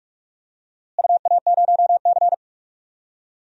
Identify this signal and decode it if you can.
Morse code, heard as a keyed tone, about 46 wpm, 700 Hz. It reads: UA0C
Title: DXCC-100 Call Signs - 4 Characters - Repeated 3x - 45wpm